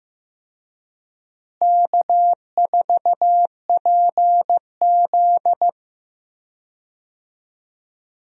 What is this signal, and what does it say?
Morse code, heard as a keyed tone, about 15 wpm, 700 Hz. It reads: K4PZ
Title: USA Call Signs - 4 Characters - Repeated 3x - 15wpm